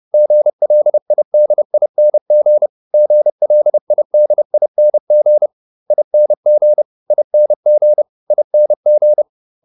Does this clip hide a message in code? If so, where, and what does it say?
Yes: Morse code from 0.1 to 9.2 s, heard as a keyed tone, about 30 wpm, 600 Hz. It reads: GLIDING GLIDING ING ING ING